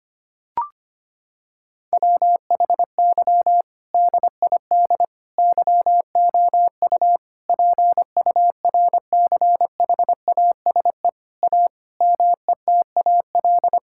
Morse code, heard as a keyed tone, about 25 wpm, 700 Hz. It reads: WHY DID YOU PURCHASE A METAL
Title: Sentences from Top 1000 Words - Review Part 6 - 25wpm